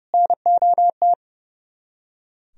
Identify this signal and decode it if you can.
Morse code, heard as a keyed tone, about 40 wpm, spaced as for 35 wpm, 700 Hz. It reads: NOT
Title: Speed-Racing - Sets of 3 Words 30wpm